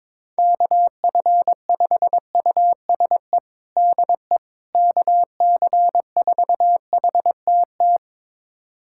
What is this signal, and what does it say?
Morse code, heard as a keyed tone, about 22 wpm, 700 Hz. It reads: KF5USE DE KC4HTT
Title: USA Callsign DE Callsign 22wpm